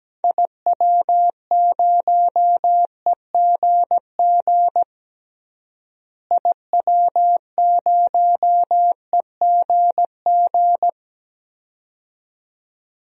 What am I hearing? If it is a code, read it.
Morse code, heard as a keyed tone, about 17 wpm, 700 Hz. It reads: IW0EGG IW0EGG
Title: DXCC-100 Call Signs - 6 Characters - Repeated 2x - 17wpm